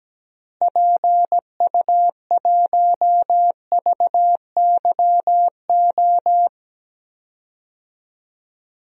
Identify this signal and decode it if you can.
Morse code, heard as a keyed tone, about 17 wpm, 700 Hz. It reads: PU1VYO
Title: DXCC-100 Call Signs - 6 Characters - Repeated 2x - 17wpm